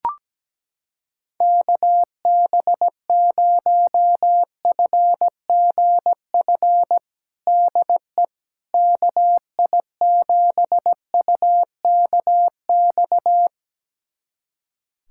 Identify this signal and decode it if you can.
Morse code, heard as a keyed tone, about 17 wpm, 700 Hz. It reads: KB0FGF DE KI7UKX
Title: USA Callsign DE Callsign - 6 Characters - 17wpm